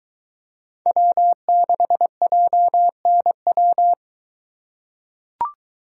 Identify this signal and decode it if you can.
Morse code, heard as a keyed tone, about 22 wpm, 700 Hz. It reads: W6JNW